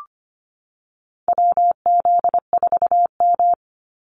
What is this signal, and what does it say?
Morse code, heard as a keyed tone, about 25 wpm, 700 Hz. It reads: WZ4M
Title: NA SOTA Activator Callsigns 25wpm